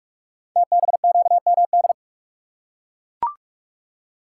Morse code, heard as a keyed tone, about 44 wpm, 700 Hz. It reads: TBQMD
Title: Code Groups - 5 Letters - 45wpm